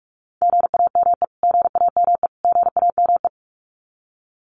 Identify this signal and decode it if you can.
Morse code, heard as a keyed tone, about 44 wpm, 700 Hz. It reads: GAME GAME GAME
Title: Top 500 Words Repeated 45wpm